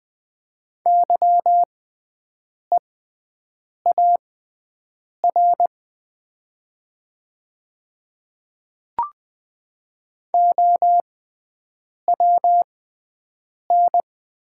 Morse code, heard as a keyed tone, about 21 wpm, spaced as for 7 wpm, 700 Hz. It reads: YEAR OWN
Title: 6x Character Spacing - Top 300 Words - 20wpm